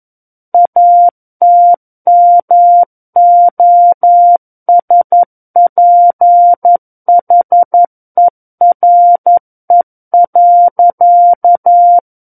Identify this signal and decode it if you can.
Morse code, heard as a keyed tone, about 11 wpm, 700 Hz. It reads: ATMOSPHERE.
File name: 11 WPM